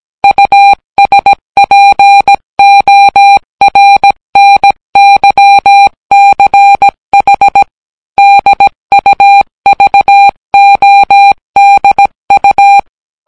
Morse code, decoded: USPORNYCH DUVODU